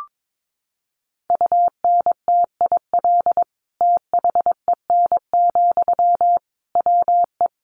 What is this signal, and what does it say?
Morse code, heard as a keyed tone, about 22 wpm, 700 Hz. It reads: UNTIL THEN, WE
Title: Sentences from Top 1500 Words 22wpm